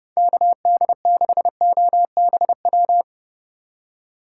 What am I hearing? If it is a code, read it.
Morse code, heard as a keyed tone, about 37 wpm, 700 Hz. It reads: KD6OBW
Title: Speed Racing - USA 2x3 Call Signs 30wpm